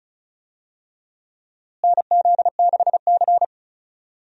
Morse code, heard as a keyed tone, about 34 wpm, 700 Hz. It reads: NZ6C
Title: USA Call Signs - 4 Characters - Repeated 2x - 35wpm